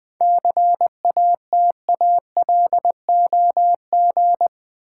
Morse code, heard as a keyed tone, about 20 wpm, 700 Hz. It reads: CATALOG
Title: 7 Letter Words - First 3 Letters Given - 20wpm